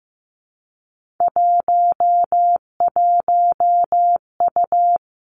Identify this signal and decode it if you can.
Morse code, heard as a keyed tone, about 15 wpm, 700 Hz. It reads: 11U
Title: Three Challenging Letter-Number Combinations 15wpm